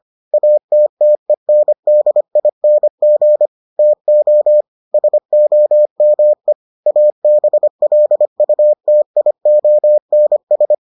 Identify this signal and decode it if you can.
Morse code, heard as a keyed tone, about 25 wpm, 600 Hz. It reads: ATTENDING TO SOME ABLUTIONS